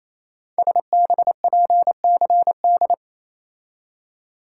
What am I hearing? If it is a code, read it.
Morse code, heard as a keyed tone, about 27 wpm, 700 Hz. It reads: SBPCD
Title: Code Groups - 5 Letters - 28wpm